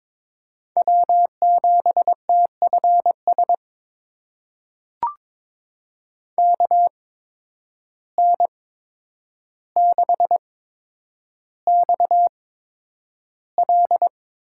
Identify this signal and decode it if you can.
Morse code, heard as a keyed tone, about 22 wpm, spaced as for 9 wpm, 700 Hz. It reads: W7TFS KN6XL
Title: USA Call Signs - 5 Characters - 8x Character Spacing - 22wpm